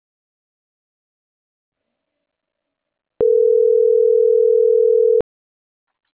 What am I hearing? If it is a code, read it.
Ringback tone (North America), 440 + 480 Hz.